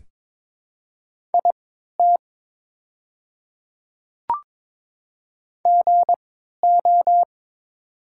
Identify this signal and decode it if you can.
Morse code, heard as a keyed tone, about 22 wpm, spaced as for 12 wpm, 700 Hz. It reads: IT GO
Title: 3x Character Spacing - Top 100 Words - 22wpm